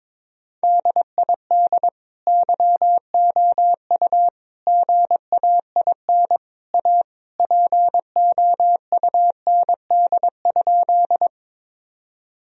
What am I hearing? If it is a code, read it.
Morse code, heard as a keyed tone, about 22 wpm, 700 Hz. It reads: DID YOU GAIN A POUND?
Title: Sentences from Top 1000 Words - Review Part 2 - 22wpm